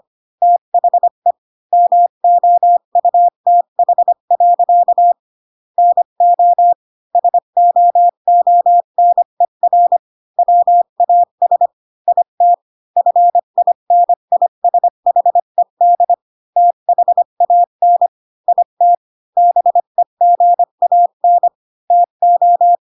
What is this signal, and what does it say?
Morse code, heard as a keyed tone, about 25 wpm, 700 Hz. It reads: THE MOUTH. NO SOONER WAS IT FINISHED THAN IT BEGAN TO
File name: Pinocchio / Pinocchio ch3 20 25 ik1hge